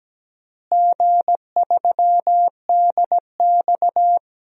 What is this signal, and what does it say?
Morse code, heard as a keyed tone, about 17 wpm, 700 Hz. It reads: G3DX
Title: DXCC-100 Call Signs - 4 Characters - Repeated 3x - 17wpm